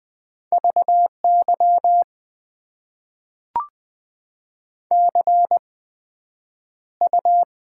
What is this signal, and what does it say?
Morse code, heard as a keyed tone, about 20 wpm, spaced as for 9 wpm, 700 Hz. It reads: VY CU
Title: Top 90 QSO Elements - 8x Character Spacing - 20wpm